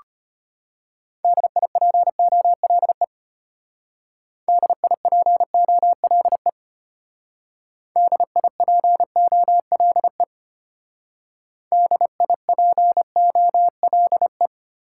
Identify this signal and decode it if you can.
Morse code, heard as a keyed tone, about 31 wpm, 700 Hz. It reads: DIPOLE DIPOLE DIPOLE DIPOLE
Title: Speed-Racing Top 90 QSO Elements 25wpm